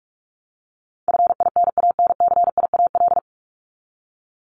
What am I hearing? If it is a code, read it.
Morse code, heard as a keyed tone, about 44 wpm, 700 Hz. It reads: FINANCIAL
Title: Top 751-1000 Words 45wpm